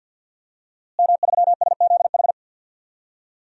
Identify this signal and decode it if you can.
Morse code, heard as a keyed tone, about 49 wpm, 700 Hz. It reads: M3SZH
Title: DXCC-100 Call Signs - 5 Characters - Repeated 3x - 50wpm